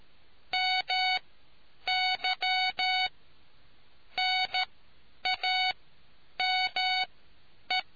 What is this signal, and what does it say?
Morse code, heard as a keyed tone, about 13 wpm, spaced as for 8 wpm, 2250 Hz. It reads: MY NAME